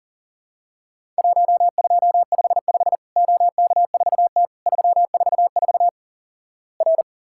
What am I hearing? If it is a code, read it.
Morse code, heard as a keyed tone, about 38 wpm, 700 Hz. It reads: 1255 OK4T 344 R
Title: Worked All Europe DX Contest - QTC Exchange - 40wpm